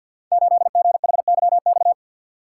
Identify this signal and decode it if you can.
Morse code, heard as a keyed tone, about 52 wpm, 700 Hz. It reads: 8GHYX
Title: Code Groups - 5 Letter:Numbers - 50wpm